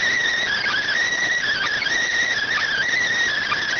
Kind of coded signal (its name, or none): Emergency Alert System (SAME) tones